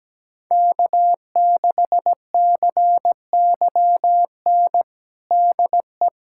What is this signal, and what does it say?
Morse code, heard as a keyed tone, about 17 wpm, 700 Hz. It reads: K6CYN DE
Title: USA Callsign DE Callsign - 5 Characters - 17wpm